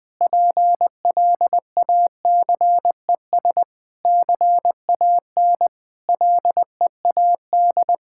Morse code, heard as a keyed tone, about 20 wpm, 700 Hz. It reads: PLACES CAN LEAD
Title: Difficult Sentences with Hints - 20wpm